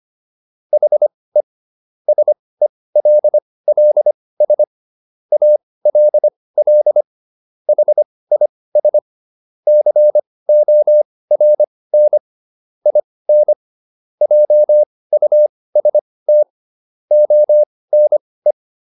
Morse code, heard as a keyed tone, about 25 wpm, spaced as for 18 wpm, 600 Hz. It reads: HE SELLS ALL HIS CORN IN JUST ONE